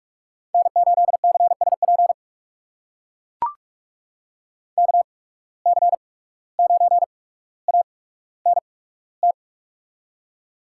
Morse code, heard as a keyed tone, about 41 wpm, spaced as for 18 wpm, 700 Hz. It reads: N8CSP KC9ANT
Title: USA Call Signs - 8x Character Spacing - 45wpm